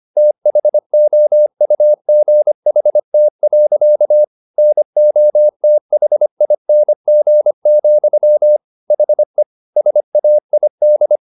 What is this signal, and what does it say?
Morse code, heard as a keyed tone, about 25 wpm, 600 Hz. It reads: THOUGHT. NOTHING, HE SAID